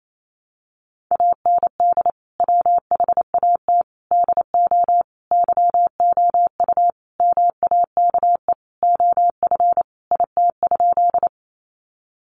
Morse code, heard as a keyed tone, about 28 wpm, 700 Hz. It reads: AND WHAT DO YOU MAKE OF IT?
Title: Sentences from Top 100 Words 28wpm